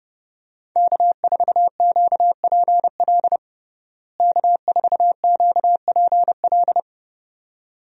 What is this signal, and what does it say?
Morse code, heard as a keyed tone, about 30 wpm, 700 Hz. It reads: K4QPL K4QPL
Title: USA Call Signs - 5 Characters - Repeated 2x - 30wpm